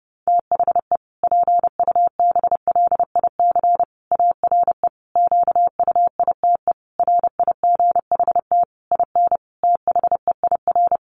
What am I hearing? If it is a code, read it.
Morse code, heard as a keyed tone, about 30 wpm, 700 Hz. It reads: THE PUBLIC ARE QUITE RIGHT IN THEIR
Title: Sentences from Top 1300 Words 30wpm